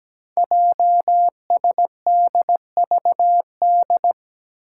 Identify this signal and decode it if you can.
Morse code, heard as a keyed tone, about 17 wpm, 700 Hz. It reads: JSDVD
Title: Code Groups - 5 Letters - 17wpm